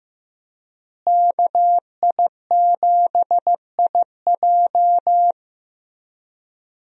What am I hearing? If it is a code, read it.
Morse code, heard as a keyed tone, about 15 wpm, 700 Hz. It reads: KI7IJ
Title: USA Call Signs - 5 Characters - Repeated 3x - 15wpm